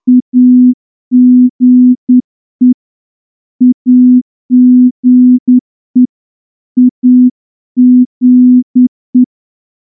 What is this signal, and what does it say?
Morse code, heard as a keyed tone, about 10 wpm, 250 Hz. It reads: AGE AGE AGE
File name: morse-training-10WPM